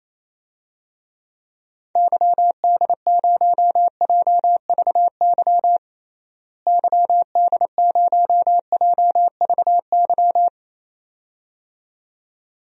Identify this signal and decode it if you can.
Morse code, heard as a keyed tone, about 28 wpm, 700 Hz. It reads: YD0JVY YD0JVY
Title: DXCC-100 Call Signs - 6 Characters - Repeated 2x - 28wpm